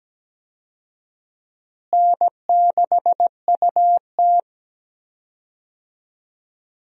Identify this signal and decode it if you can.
Morse code, heard as a keyed tone, about 17 wpm, 700 Hz. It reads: N6UT
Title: USA Call Signs - 4 Characters - Repeated 2x - 17wpm